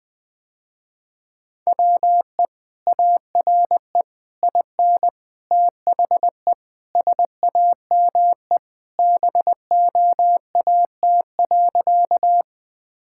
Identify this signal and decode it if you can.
Morse code, heard as a keyed tone, about 20 wpm, 700 Hz. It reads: WE ARE IN THE SAME BOAT.
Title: Sentences with 3-Word Phrases - Spoken Beginning - 20wpm